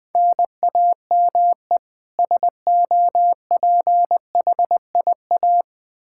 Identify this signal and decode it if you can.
Morse code, heard as a keyed tone, about 20 wpm, 700 Hz. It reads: NAME SOPHIA
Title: Practice QSO - Exchange 1 - Beginning - 20wpm